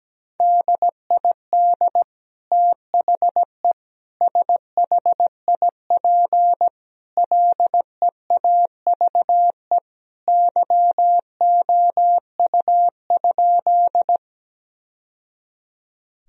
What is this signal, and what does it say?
Morse code, heard as a keyed tone, about 17 wpm, 700 Hz. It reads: DID THE SHIP LEAVE YOU?
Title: Sentences from Top 1000 Words - Review Part 2 - 17wpm